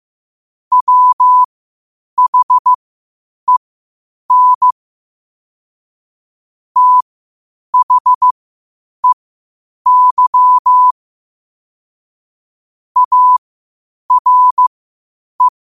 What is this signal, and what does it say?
Morse code, heard as a keyed tone, about 15 wpm, spaced as for 8 wpm, 1000 Hz. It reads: WHEN THEY ARE